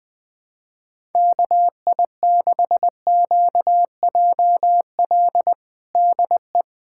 Morse code, heard as a keyed tone, about 20 wpm, 700 Hz. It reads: KI6QJL DE
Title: USA Callsign DE Callsign - 6 Characters - 20wpm